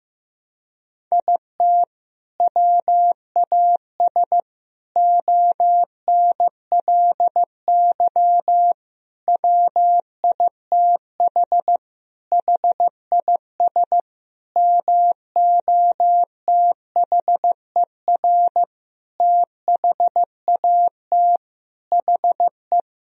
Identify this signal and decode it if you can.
Morse code, heard as a keyed tone, about 15 wpm, 700 Hz. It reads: IT WAS ONLY WITH HIS MOTHER THAT HE
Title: Sentences from Top 1000 Words - Review Part 1 - 15wpm